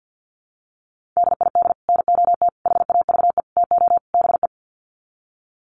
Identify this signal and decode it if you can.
Morse code, heard as a keyed tone, about 49 wpm, 700 Hz. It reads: DID NOT HAVE TO BE
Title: Sets of 5 Words Encore2 50wpm